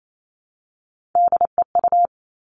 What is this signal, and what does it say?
Morse code, heard as a keyed tone, about 28 wpm, 700 Hz. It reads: DEU
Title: Three Challenging Letter-Number Combinations 28wpm